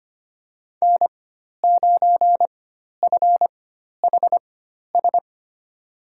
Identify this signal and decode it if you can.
Morse code, heard as a keyed tone, about 25 wpm, spaced as for 12 wpm, 700 Hz. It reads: N9FHS
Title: USA Call Signs - 5 Characters - 4x Character Spacing - 25wpm